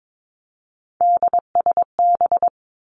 Morse code, heard as a keyed tone, about 22 wpm, 700 Hz. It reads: DSB
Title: Three Challenging Letter-Number Combinations 22wpm